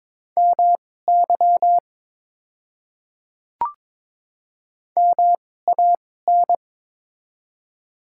Morse code, heard as a keyed tone, about 22 wpm, spaced as for 16 wpm, 700 Hz. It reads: MY MAN
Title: 2x Character Spacing - Top 200 Words - 22wpm